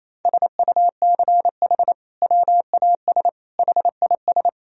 Morse code, heard as a keyed tone, about 28 wpm, 700 Hz. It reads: SUCH WAS HIS
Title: Sentences from Top 1000 Words - Review Part 1 - 28wpm